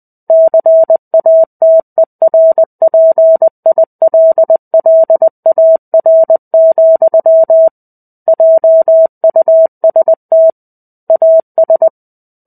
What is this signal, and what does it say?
Morse code, heard as a keyed tone, about 20 wpm, 650 Hz. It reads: CATERPILLAR, JUST AS
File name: 05-Chapter-05 20 WPM